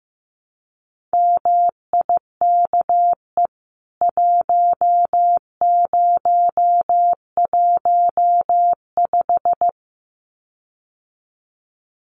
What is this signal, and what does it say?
Morse code, heard as a keyed tone, about 15 wpm, 700 Hz. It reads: MIKE 1015
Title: CWT Exchange 15wpm